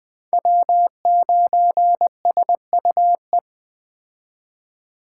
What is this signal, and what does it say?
Morse code, heard as a keyed tone, about 20 wpm, 700 Hz. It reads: W9SUE
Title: Speed Racing - USA 1x3 Call Signs 17wpm